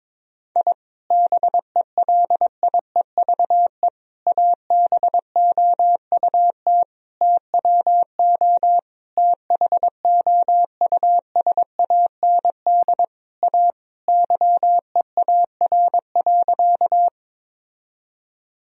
Morse code, heard as a keyed tone, about 22 wpm, 700 Hz. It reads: I BELIEVE ABOUT TWO THOUSAND A YEAR.
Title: Sentences from Top 1000 Words - Review Part 3 - 22wpm